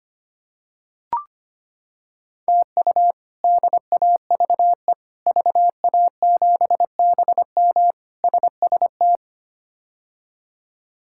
Morse code, heard as a keyed tone, about 25 wpm, 700 Hz. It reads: TU DAVE VA7BM SST
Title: K1USN SST QSO Practice - 25wpm